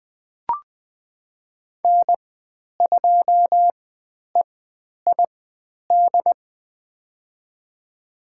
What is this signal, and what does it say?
Morse code, heard as a keyed tone, about 20 wpm, spaced as for 10 wpm, 700 Hz. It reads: N2EID
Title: 095 Lesson 16g - Letter c - Call Signs 20wpm